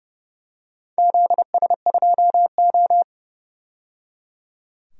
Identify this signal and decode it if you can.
Morse code, heard as a keyed tone, about 30 wpm, 700 Hz. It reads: ZS2O
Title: DXCC-100 Call Signs - 4 Characters - 30wpm